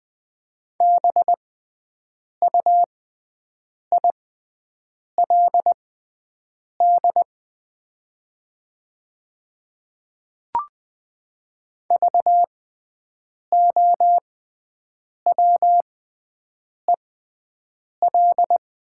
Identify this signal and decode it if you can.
Morse code, heard as a keyed tone, about 20 wpm, spaced as for 7 wpm, 700 Hz. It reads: BUILD VOWEL
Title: 6x Character Spacing - Top 500 Words - 20wpm